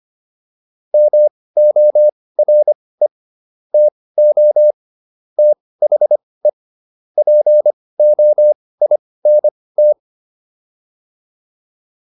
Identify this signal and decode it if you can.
Morse code, heard as a keyed tone, about 25 wpm, spaced as for 18 wpm, 600 Hz. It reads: MORE TO THE POINT
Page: intermediate-practice-files